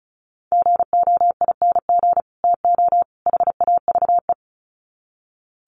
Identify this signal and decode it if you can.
Morse code, heard as a keyed tone, about 34 wpm, 700 Hz. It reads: GOING TO HAVE
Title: Sets of 3 Words 35wpm